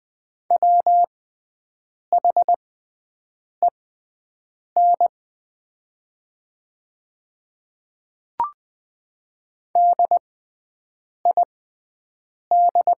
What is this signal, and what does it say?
Morse code, heard as a keyed tone, about 20 wpm, spaced as for 7 wpm, 700 Hz. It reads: WHEN DID
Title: 6x Character Spacing - Top 100 Words - 20wpm